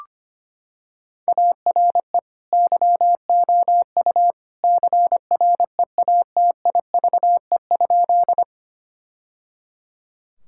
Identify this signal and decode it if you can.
Morse code, heard as a keyed tone, about 25 wpm, 700 Hz. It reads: ARE YOU CREATIVE?
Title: Sentences from Top 1000 Words - Review Part 5 - 25wpm